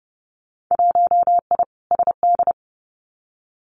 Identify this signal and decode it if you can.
Morse code, heard as a keyed tone, about 30 wpm, 700 Hz. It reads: 1I SD
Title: Winter Field Day Exchanges 30wpm